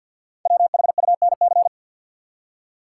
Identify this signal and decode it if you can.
Morse code, heard as a keyed tone, about 46 wpm, 700 Hz. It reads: WHUNC
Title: Code Groups - 5 Letters - 50wpm